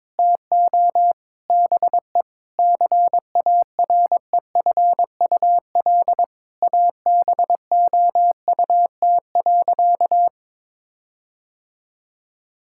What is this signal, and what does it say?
Morse code, heard as a keyed tone, about 22 wpm, 700 Hz. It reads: TO BE CAREFUL ABOUT.
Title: Sentences from Top 1600 Words 22wpm